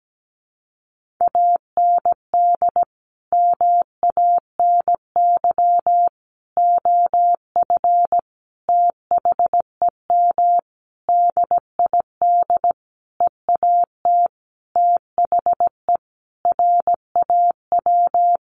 Morse code, heard as a keyed tone, about 17 wpm, 700 Hz. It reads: AND MANY OF THEM DID EAT THE RAW
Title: Sentences from Top 1500 Words 17wpm